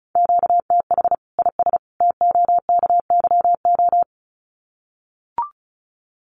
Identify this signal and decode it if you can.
Morse code, heard as a keyed tone, about 34 wpm, 700 Hz. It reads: QTH IS TOKYO
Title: Common QSO Phrases 35wpm